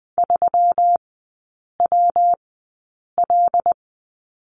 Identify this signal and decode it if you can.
Morse code, heard as a keyed tone, about 20 wpm, spaced as for 8 wpm, 700 Hz. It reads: 3WL
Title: 166 Lesson 27e - Number 7 - Three Characters 20wpm